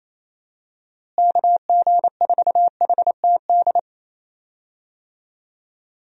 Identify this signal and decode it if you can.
Morse code, heard as a keyed tone, about 28 wpm, 700 Hz. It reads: KG4HTD
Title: USA Call Signs - 6 Characters - Repeated 2x - 28wpm